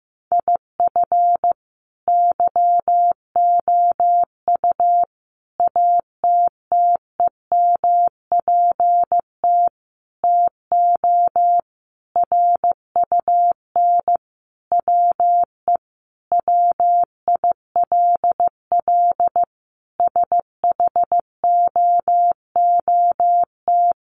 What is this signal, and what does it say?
Morse code, heard as a keyed tone, about 15 wpm, 700 Hz. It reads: IF YOU ATTEMPT TO RUN WE WILL SHOOT
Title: Sentences from Top 1200 Words 15wpm